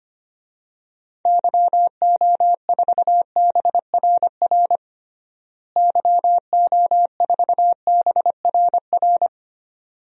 Morse code, heard as a keyed tone, about 25 wpm, 700 Hz. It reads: YO4BRR YO4BRR
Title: DXCC-100 Call Signs - 6 Characters - Repeated 2x - 25wpm